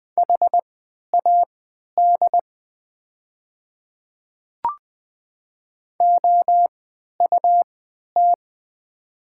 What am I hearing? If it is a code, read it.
Morse code, heard as a keyed tone, about 20 wpm, spaced as for 11 wpm, 700 Hz. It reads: HAD OUT
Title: 3x Character Spacing - Top 100 Words - 20wpm